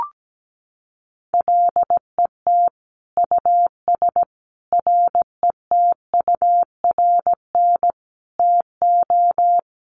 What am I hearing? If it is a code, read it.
Morse code, heard as a keyed tone, about 17 wpm, 700 Hz. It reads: LET US RETURN TO
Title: Sentences from Top 800 Words 17wpm